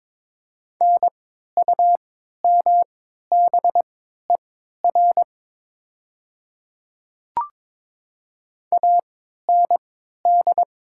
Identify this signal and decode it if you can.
Morse code, heard as a keyed tone, about 22 wpm, spaced as for 13 wpm, 700 Hz. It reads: NUMBER AND
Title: 3x Character Spacing - Top 100 Words - 22wpm